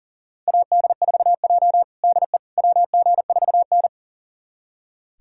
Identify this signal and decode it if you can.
Morse code, heard as a keyed tone, about 40 wpm, 700 Hz. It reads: AD4J DE WG4N
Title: USA Callsign DE Callsign - 4 Characters - 40wpm